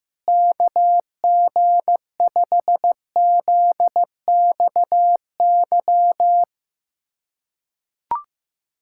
Morse code, heard as a keyed tone, about 15 wpm, 700 Hz. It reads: KG5ZXY
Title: USA Callsign DE Callsign 15wpm